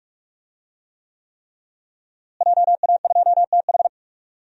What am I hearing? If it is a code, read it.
Morse code, heard as a keyed tone, about 42 wpm, 700 Hz. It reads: JA2TH
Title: DXCC-100 Call Signs - 5 Characters - Repeated 3x - 45wpm